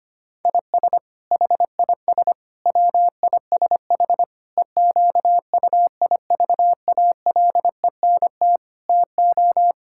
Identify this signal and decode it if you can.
Morse code, heard as a keyed tone, about 25 wpm, 700 Hz. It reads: IS HIS WISH EQUIVALENT TO
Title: Sentences from Top 1600 Words 25wpm